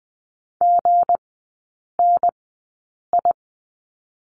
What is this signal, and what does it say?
Morse code, heard as a keyed tone, about 20 wpm, spaced as for 8 wpm, 690 Hz. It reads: GNI
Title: 159 Lesson 26e - Letter g - Three Characters 20wpm